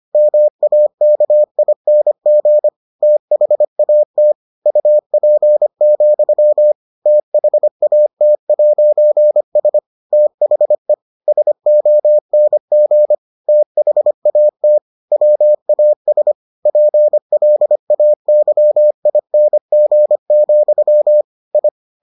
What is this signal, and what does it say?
Morse code, heard as a keyed tone, about 25 wpm, 600 Hz. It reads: MAKING THAT UP, THAT'S THE SONG THAT WAS PLAYING, I